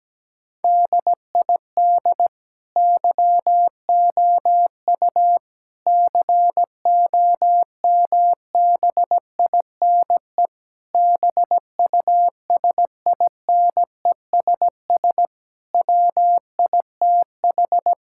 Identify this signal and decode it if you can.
Morse code, heard as a keyed tone, about 17 wpm, 700 Hz. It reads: DID YOU COMBINE BUSINESS WITH
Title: Sentences from Top 1600 Words 17wpm